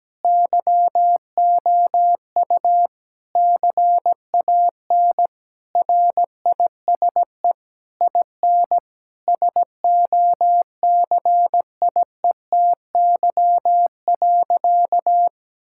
Morse code, heard as a keyed tone, about 17 wpm, 700 Hz. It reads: YOU CAN RISE IN SOCIETY.
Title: Sentences from Top 1000 Words - Review Part 6 - 17wpm